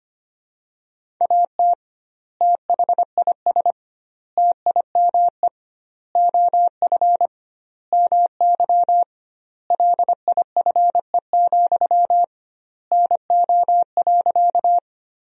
Morse code, heard as a keyed tone, about 25 wpm, 700 Hz. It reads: AT THIS TIME OF MY LIFE, NO.
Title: 2x Word-Spacing - Sentences from Top 300 Words - 25wpm